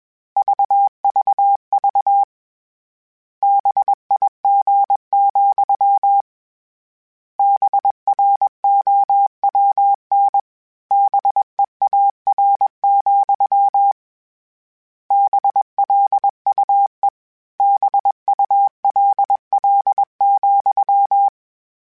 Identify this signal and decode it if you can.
Morse code, heard as a keyed tone, about 21 wpm, 800 Hz. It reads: VVV BIG, BROWN BEAR, BLUE BULL,